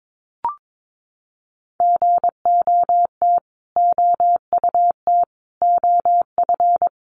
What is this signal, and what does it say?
Morse code, heard as a keyed tone, about 22 wpm, 700 Hz. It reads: GOT OUT OF
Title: Sets of 5 Words Encore 22wpm